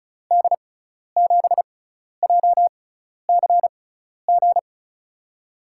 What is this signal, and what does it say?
Morse code, heard as a keyed tone, about 36 wpm, spaced as for 12 wpm, 700 Hz. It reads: D7JCG
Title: USA Call Signs - 6x Character Spacing - 35wpm